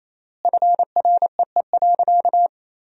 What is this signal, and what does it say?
Morse code, heard as a keyed tone, about 28 wpm, 700 Hz. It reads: FREE.
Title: 3x Word-Spacing - Sentences from Top 500 Words - 28wpm